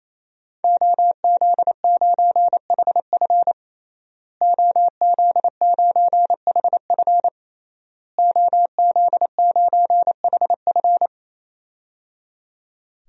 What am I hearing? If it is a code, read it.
Morse code, heard as a keyed tone, about 28 wpm, 700 Hz. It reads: OZ9HF OZ9HF OZ9HF
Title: DXCC-100 Call Signs - 5 Characters - Repeated 3x - 28wpm